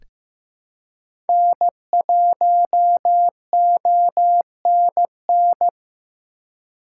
Morse code, heard as a keyed tone, about 15 wpm, 700 Hz. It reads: N1ONN
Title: USA Call Signs - 5 Characters - 2x Character Spacing - 15wpm